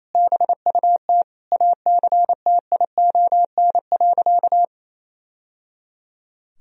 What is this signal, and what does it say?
Morse code, heard as a keyed tone, about 27 wpm, 700 Hz. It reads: BUT ACTION.
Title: Sentences from Top 1000 Words - Review Part 4 - 28wpm